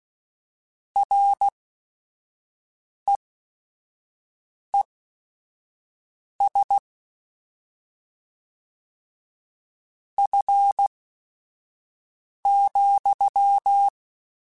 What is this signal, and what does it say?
Morse code, heard as a keyed tone, about 16 wpm, spaced as for 5 wpm, 800 Hz. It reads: REES F,
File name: Weather Sets / Weather Sets-mp3s / Weather 5 12 @5